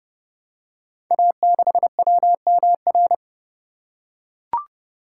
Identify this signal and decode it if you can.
Morse code, heard as a keyed tone, about 30 wpm, 700 Hz. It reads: A6WMR
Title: Code Groups - 5 Letter:Numbers - 30wpm